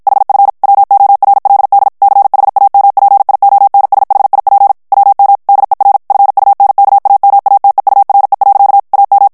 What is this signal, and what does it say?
Morse code, heard as a keyed tone, about 55 wpm, 800 Hz. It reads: HF WORLD CHAMPIONSHIP WM DER FUNKAMATEURE. AM
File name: morse mp3 / ALLEMAND texte clair 2 vitesse 50 mots minute